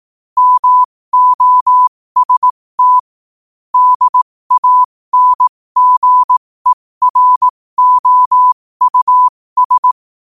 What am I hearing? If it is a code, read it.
Morse code, heard as a keyed tone, about 18 wpm, spaced as for 15 wpm, 1000 Hz. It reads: MOST DANGEROUS